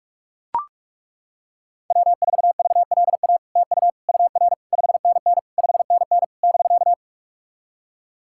Morse code, heard as a keyed tone, about 43 wpm, 700 Hz. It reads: W4VLA TU UR 5NN 5NN <BK>
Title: SOTA QSO Practice 45wpm